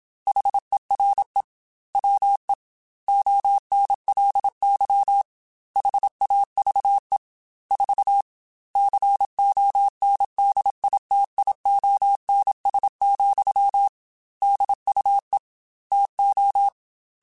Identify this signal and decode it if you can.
Morse code, heard as a keyed tone, about 27 wpm, 800 Hz. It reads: HERE WE ONLY HAVE 4 CONDITIONS, DUE TO
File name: Technical Subjects / Technical Subjects-mp3s / AND Gate @25